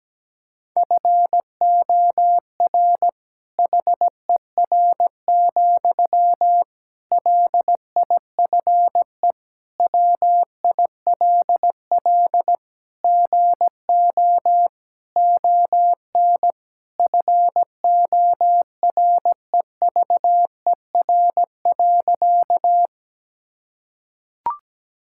Morse code, heard as a keyed tone, about 17 wpm, 700 Hz. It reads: FOR HER, LIFE WILL GO ON FOREVER.
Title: Sentences from Top 1600 Words 17wpm